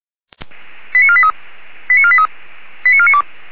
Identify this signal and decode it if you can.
Five-tone selective call (ZVEI), heard as quick runs of five tones, tones of about 70 ms. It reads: 89483 89483 89492